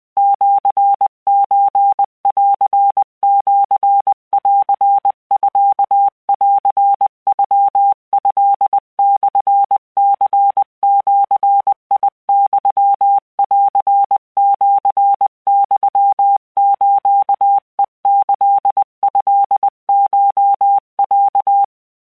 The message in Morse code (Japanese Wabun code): シソンシンミンノトモニジユンシユスヘキトコロ